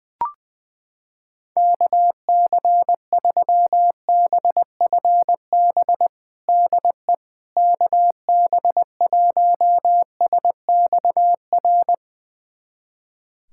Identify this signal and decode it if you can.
Morse code, heard as a keyed tone, about 20 wpm, 700 Hz. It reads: KC3BFB DE KB1SXR
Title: USA Callsign DE Callsign - 6 Characters - 20wpm